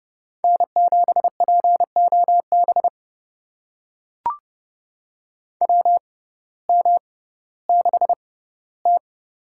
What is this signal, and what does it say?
Morse code, heard as a keyed tone, about 30 wpm, spaced as for 17 wpm, 700 Hz. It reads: N7POB WM6T
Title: USA Call Signs - 6x Character Spacing - 30wpm